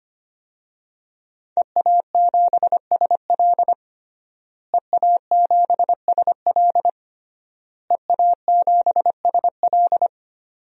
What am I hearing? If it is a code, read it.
Morse code, heard as a keyed tone, about 25 wpm, 700 Hz. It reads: EA7SL EA7SL EA7SL
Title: DXCC-100 Call Signs - 5 Characters - Repeated 3x - 25wpm